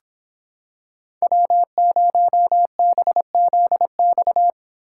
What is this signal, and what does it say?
Morse code, heard as a keyed tone, about 26 wpm, 700 Hz. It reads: W0BZX